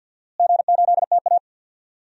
Morse code, heard as a keyed tone, about 52 wpm, 700 Hz. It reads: G8TA
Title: DXCC-100 Call Signs - 4 Characters - Repeated 3x - 50wpm